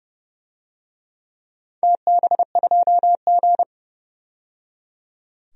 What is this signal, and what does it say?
Morse code, heard as a keyed tone, about 29 wpm, 700 Hz. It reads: TB2G
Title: DXCC-100 Call Signs - 4 Characters - 30wpm